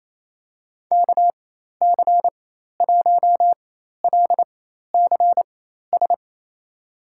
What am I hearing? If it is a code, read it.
Morse code, heard as a keyed tone, about 27 wpm, spaced as for 13 wpm, 700 Hz. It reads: KC1LCS
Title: USA Call Signs - 6 Characters - 4x Character Spacing - 28wpm